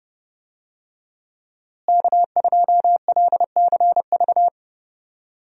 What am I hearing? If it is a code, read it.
Morse code, heard as a keyed tone, about 30 wpm, 700 Hz. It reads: K2LCV
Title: USA Call Signs - 5 Characters - Repeated 2x - 30wpm